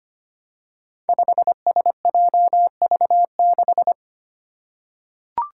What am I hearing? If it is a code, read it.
Morse code, heard as a keyed tone, about 25 wpm, 700 Hz. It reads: 5SJV6
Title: Five Challenging Letter-Number Combinations 25wpm